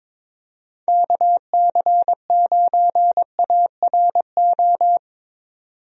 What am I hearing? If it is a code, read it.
Morse code, heard as a keyed tone, about 21 wpm, 700 Hz. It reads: KC9ARO
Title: USA Call Signs - 6 Characters - 22wpm